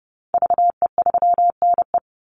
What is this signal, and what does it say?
Morse code, heard as a keyed tone, about 29 wpm, 700 Hz. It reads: VE3NE
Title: Call Signs Repeated 2x - 30wpm